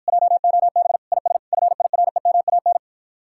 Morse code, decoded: JOB IS FIREMAN

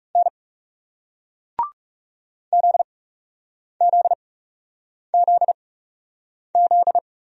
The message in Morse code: N ZZZZ

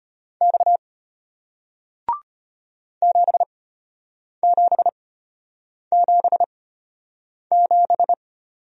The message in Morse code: X 7777

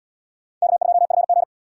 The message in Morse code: K2FG

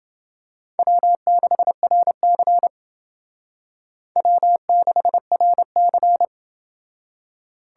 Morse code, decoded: W6RC W6RC

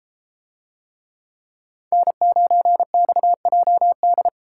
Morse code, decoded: N9XJD

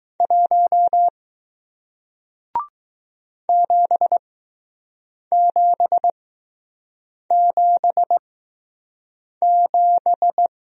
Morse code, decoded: 1 7777

